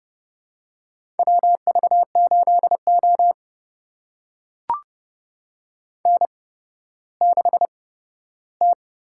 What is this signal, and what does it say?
Morse code, heard as a keyed tone, about 30 wpm, spaced as for 14 wpm, 700 Hz. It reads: WV8O N6T
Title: USA Call Signs - 4 Characters - 8x Character Spacing - 30wpm